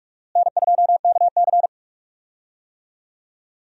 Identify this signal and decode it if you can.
Morse code, heard as a keyed tone, about 42 wpm, 700 Hz. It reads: N2KC
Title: USA Call Signs - 4 Characters - Repeated 3x - 45wpm